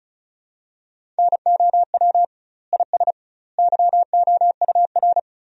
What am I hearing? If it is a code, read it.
Morse code, heard as a keyed tone, about 36 wpm, 700 Hz. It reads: NOW IS YOUR
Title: 2x Word-Spacing - Sentences from Top 100 Words - 35wpm